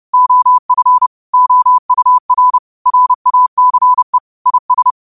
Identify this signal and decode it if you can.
Morse code, heard as a keyed tone, about 30 wpm, 1000 Hz. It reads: OF OUR RACE IS
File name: qotdfiles / 30wpm